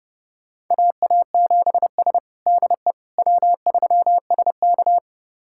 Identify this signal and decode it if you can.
Morse code, heard as a keyed tone, about 30 wpm, 700 Hz. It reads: AA7S DE W3SK
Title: USA Callsign DE Callsign - 4 Characters - 30wpm